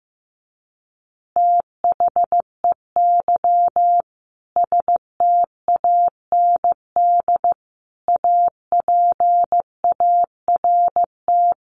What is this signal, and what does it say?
Morse code, heard as a keyed tone, about 14 wpm, 700 Hz. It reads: THEY STAND APART
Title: Easy Phrases 15wpm